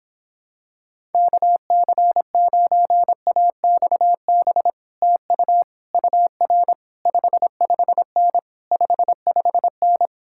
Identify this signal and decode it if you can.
Morse code, heard as a keyed tone, about 26 wpm, 700 Hz. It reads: KC9AXB TU UR 55N 55N